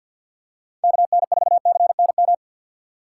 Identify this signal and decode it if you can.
Morse code, heard as a keyed tone, about 53 wpm, 700 Hz. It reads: KN4CNM
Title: USA Call Signs - 6 Characters - 4x Character Spacing - 50wpm